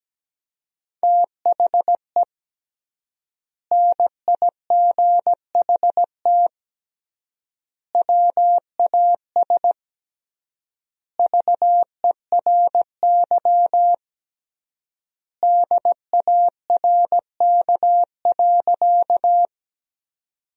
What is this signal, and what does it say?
Morse code, heard as a keyed tone, about 17 wpm, 700 Hz. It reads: THE NIGHT WAS VERY DARK.
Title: 3x Word-Spacing - Sentences from Top 500 Words - 17wpm